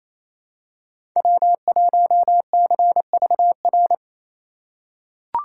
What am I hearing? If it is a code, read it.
Morse code, heard as a keyed tone, about 28 wpm, 700 Hz. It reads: W1CVR
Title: USA Call Signs - 5 Characters - 4x Character Spacing - 28wpm